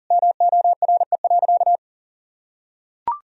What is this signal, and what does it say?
Morse code, heard as a keyed tone, about 37 wpm, 700 Hz. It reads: MORE.
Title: Sentences from Top 1000 Words - Review Part 4 - 40wpm